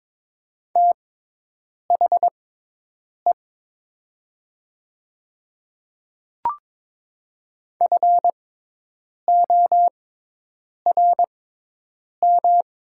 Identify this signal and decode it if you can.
Morse code, heard as a keyed tone, about 22 wpm, spaced as for 8 wpm, 700 Hz. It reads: THE FORM
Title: 6x Character Spacing - Top 200 Words - 22wpm